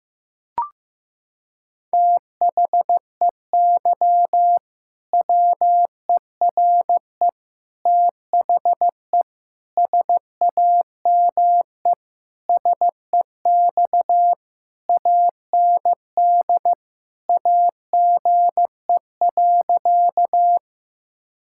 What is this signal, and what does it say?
Morse code, heard as a keyed tone, about 15 wpm, 700 Hz. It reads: THEY WERE THE SAME SEX AND AGE.
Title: Sentences from Top 1600 Words 15wpm